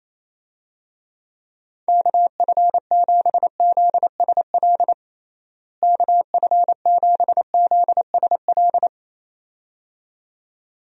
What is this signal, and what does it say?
Morse code, heard as a keyed tone, about 27 wpm, 700 Hz. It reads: KF7ZSL KF7ZSL
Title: USA Call Signs - 6 Characters - Repeated 2x - 28wpm